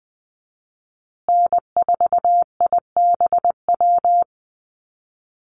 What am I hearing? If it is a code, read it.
Morse code, heard as a keyed tone, about 20 wpm, 700 Hz. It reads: N4IBW
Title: USA Call Signs Repeated 3x 20wpm